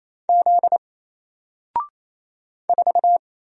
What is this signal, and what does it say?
Morse code, heard as a keyed tone, about 28 wpm, 700 Hz. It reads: Z 4
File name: Single-Letter-Number-28wpm